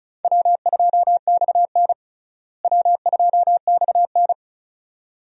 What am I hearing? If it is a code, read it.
Morse code, heard as a keyed tone, about 34 wpm, 700 Hz. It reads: W2XN W2XN
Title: USA Call Signs - 4 Characters - Repeated 2x - 35wpm